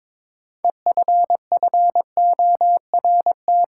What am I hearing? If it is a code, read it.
Morse code, heard as a keyed tone, about 22 wpm, 700 Hz. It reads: EFFORT
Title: Sentences from Top 1200 Words - New Word Drill - 22wpm